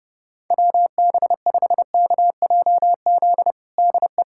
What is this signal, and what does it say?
Morse code, heard as a keyed tone, about 30 wpm, 700 Hz. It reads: WB5KJZ DE
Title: USA Callsign DE Callsign - 6 Characters - 30wpm